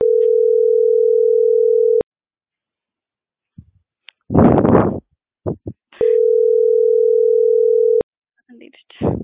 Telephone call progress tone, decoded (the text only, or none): ringback tone